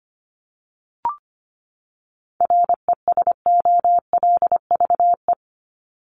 Morse code, heard as a keyed tone, about 25 wpm, 700 Hz. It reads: RESOLVE
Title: Top 1751-2000 Words 25wpm